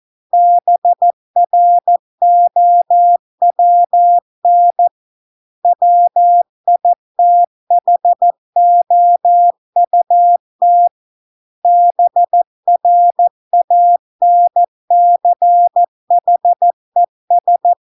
Morse code, heard as a keyed tone, about 14 wpm, 700 Hz. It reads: BROWN WITHOUT BRANCHES